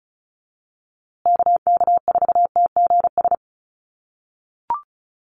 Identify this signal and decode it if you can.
Morse code, heard as a keyed tone, about 34 wpm, 700 Hz. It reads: KK4TGS
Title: USA Call Signs Repeated 3x 35wpm